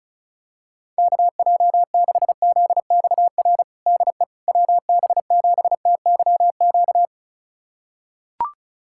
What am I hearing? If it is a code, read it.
Morse code, heard as a keyed tone, about 34 wpm, 690 Hz. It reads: KJ6ZXR DE WB7TYQ
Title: USA Callsign DE Callsign - 6 Characters - 35wpm